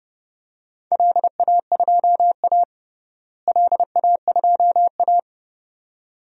Morse code, heard as a keyed tone, about 32 wpm, 700 Hz. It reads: LA2A LA2A
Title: DXCC-100 Call Signs - 4 Characters - Repeated 2x - 30wpm